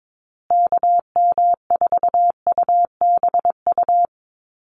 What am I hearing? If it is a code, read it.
Morse code, heard as a keyed tone, about 22 wpm, 700 Hz. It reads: KM4UBU
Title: USA Call Signs Repeated 3x 22wpm